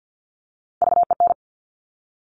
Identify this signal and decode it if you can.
Morse code, heard as a keyed tone, about 50 wpm, 700 Hz. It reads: VEN
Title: Top Three Letter Combinations in Words 50wpm